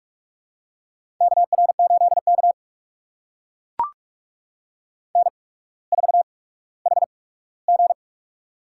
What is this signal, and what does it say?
Morse code, heard as a keyed tone, about 44 wpm, spaced as for 19 wpm, 700 Hz. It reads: KR8K N4HG
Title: USA Call Signs - 4 Characters - 8x Character Spacing - 45wpm